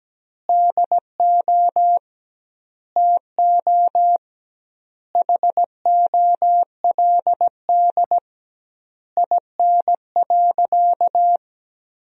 Morse code, heard as a keyed tone, about 17 wpm, 700 Hz. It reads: DO TO HOLD IN.
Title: 2x Word-Spacing - Sentences from Top 400 Words - 17wpm